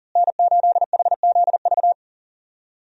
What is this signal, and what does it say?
Morse code, heard as a keyed tone, about 40 wpm, 700 Hz. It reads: N8HZV
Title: USA Call Signs - 5 Characters - Repeated 3x - 40wpm